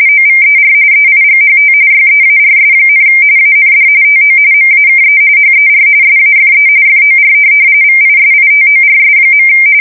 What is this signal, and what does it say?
RTTY, about 45.45 baud. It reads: OVER EIGHT YEARS, AND TO MAKE HER SAFE FOR VISITORS.
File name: fsk170 / USS-New-Jersey